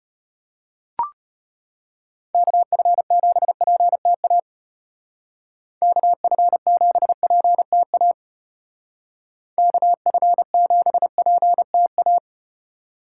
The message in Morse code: KF7PTA KF7PTA KF7PTA

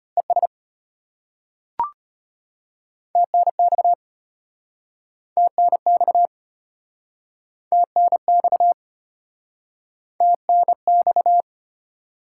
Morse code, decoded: ES TNX TNX TNX TNX

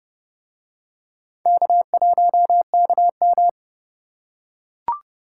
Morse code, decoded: K1KM

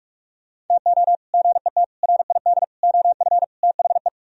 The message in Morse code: TO GET RID OF THE

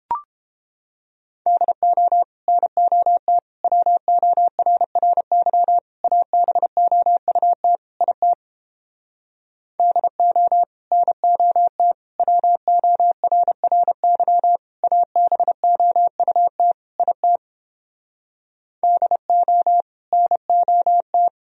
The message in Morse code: DO NOT WORRY ABOUT IT DO NOT WORRY ABOUT IT DO NOT

